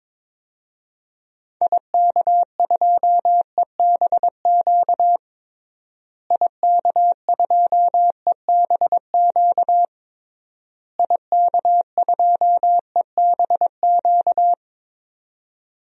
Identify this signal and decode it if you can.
Morse code, heard as a keyed tone, about 22 wpm, 700 Hz. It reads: IK2EBQ IK2EBQ IK2EBQ
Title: DXCC-100 Call Signs - 6 Characters - Repeated 3x - 22wpm